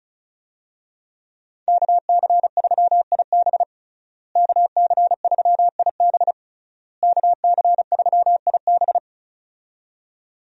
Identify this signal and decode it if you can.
Morse code, heard as a keyed tone, about 36 wpm, 700 Hz. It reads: KC3IB KC3IB KC3IB
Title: USA Call Signs - 5 Characters - Repeated 3x - 35wpm